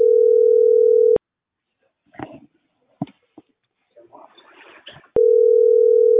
Ringback tone (North America), 440 + 480 Hz.